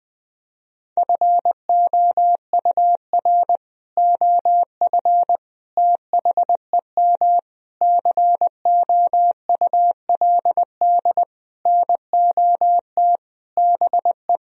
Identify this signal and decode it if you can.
Morse code, heard as a keyed tone, about 21 wpm, 700 Hz. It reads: FOUR OF THEM COULD NOT BE
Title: Sentences from Top 1000 Words - Review Part 1 - 20wpm